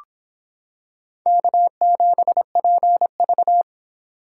Morse code, heard as a keyed tone, about 25 wpm, 700 Hz. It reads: K7PV